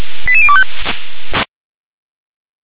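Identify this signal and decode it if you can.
Five-tone selective call (ZVEI), heard as quick runs of five tones, tones of about 70 ms. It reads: 88025
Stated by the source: ZVEI five-tone selective call 88251, received on SDR.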